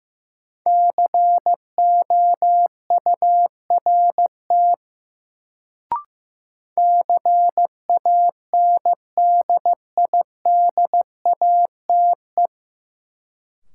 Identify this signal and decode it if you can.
Morse code, heard as a keyed tone, about 15 wpm, 700 Hz. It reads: COURT CANDIDATE
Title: Sentences from Top 1600 Words - New Word Drill Alt - 15wpm